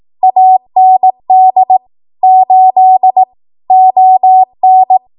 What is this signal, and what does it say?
Morse code, heard as a keyed tone, about 18 wpm, 750 Hz. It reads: AND 8 ON
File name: Archive / 18 WPM / 18WPM